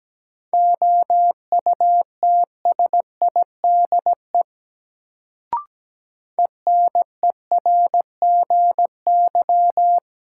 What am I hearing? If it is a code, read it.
Morse code, heard as a keyed tone, about 17 wpm, 700 Hz. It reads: OUTSIDE ENERGY
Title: Sentences from Top 900 Words - New Word Drill - 17wpm